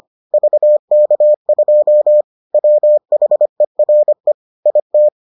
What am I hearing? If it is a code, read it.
Morse code, heard as a keyed tone, about 25 wpm, 600 Hz. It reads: VK2 WHERE IT